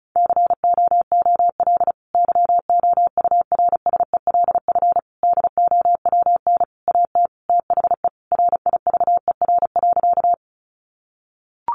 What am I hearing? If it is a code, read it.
Morse code, heard as a keyed tone, about 34 wpm, 700 Hz. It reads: COOL YOURSELF DOWN AT THE RIVER.
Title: Sentences from Top 1200 Words 35wpm